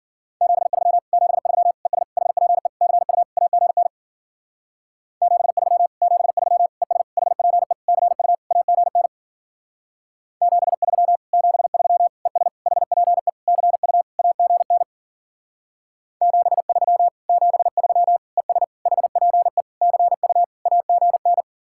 Morse code, heard as a keyed tone, about 51 wpm, 700 Hz. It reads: 73 73 ES HPE CU AGN 73 73 ES HPE CU AGN 73 73 ES HPE CU AGN 73 73 ES HPE CU AGN